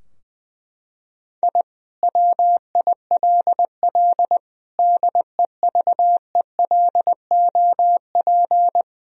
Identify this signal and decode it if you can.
Morse code, heard as a keyed tone, about 21 wpm, 700 Hz. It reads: I WILL DEVELOP
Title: Sentences from Top 1000 Words - Review Part 2 - 20wpm